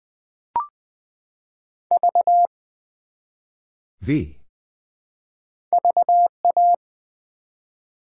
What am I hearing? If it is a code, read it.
Morse code, heard as a keyed tone, about 20 wpm, 700 Hz. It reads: V VA
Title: Build A Word - 1 Letter At A Time - 3 to 5-Letter Words - 20wpm